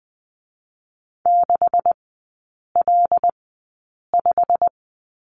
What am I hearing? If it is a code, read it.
Morse code, heard as a keyed tone, about 19 wpm, spaced as for 8 wpm, 700 Hz. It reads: 6L5
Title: 127 Lesson 21e - Number 6 - Three Characters 20wpm